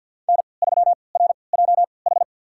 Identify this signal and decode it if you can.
Morse code, heard as a keyed tone, about 50 wpm, spaced as for 28 wpm, 700 Hz. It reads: N3RJH
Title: USA Call Signs - 5 Characters - 3x Character Spacing - 50wpm